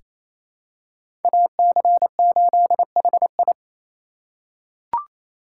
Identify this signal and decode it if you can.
Morse code, heard as a keyed tone, about 27 wpm, 700 Hz. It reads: AC8HI
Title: USA Call Signs - 5 Characters - 6x Character Spacing - 28wpm